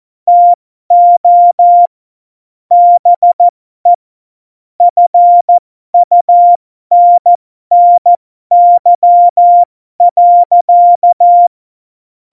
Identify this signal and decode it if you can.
Morse code, heard as a keyed tone, about 14 wpm, spaced as for 12 wpm, 700 Hz. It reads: TO BE FUNNY.
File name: Morsekurs / Witze